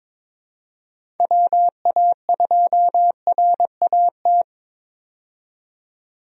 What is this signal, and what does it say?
Morse code, heard as a keyed tone, about 22 wpm, 700 Hz. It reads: WA2RAT
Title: USA Call Signs - 6 Characters - 22wpm